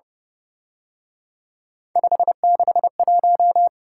Morse code, heard as a keyed tone, about 31 wpm, 700 Hz. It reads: 561
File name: Speed-Racing - Three Letter-Number Combinations - 25wpm